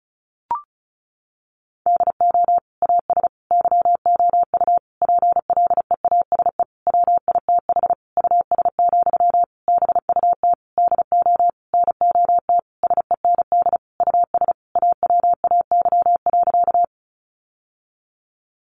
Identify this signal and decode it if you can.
Morse code, heard as a keyed tone, about 34 wpm, 700 Hz. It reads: DO AS YOU PLEASE WITH US, BUT DO NOT SEND US AWAY.
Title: Sentences from Top 1500 Words 35wpm